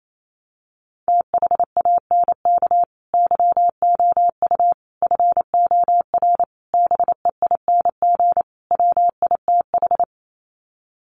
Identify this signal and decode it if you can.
Morse code, heard as a keyed tone, about 28 wpm, 700 Hz. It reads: THANK YOU FOR BEING WITH
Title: Sets of 5 Words 28wpm